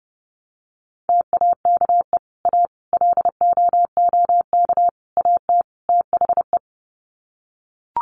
Morse code, heard as a keyed tone, about 30 wpm, 700 Hz. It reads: TAKE A LOOK AT THE
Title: Sets of 5 Words 30wpm